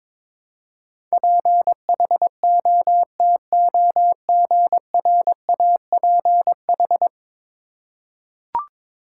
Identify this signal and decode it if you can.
Morse code, heard as a keyed tone, about 22 wpm, 700 Hz. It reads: PHOTOGRAPH
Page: morse-code-ninja-practice-22wpm